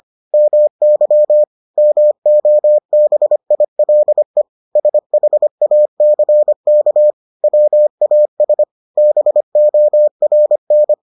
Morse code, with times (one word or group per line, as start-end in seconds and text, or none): 0.3-1.4 MY
1.8-4.4 MOBILE
4.8-7.1 SHACK
7.4-8.6 WAS
9.0-10.9 BORN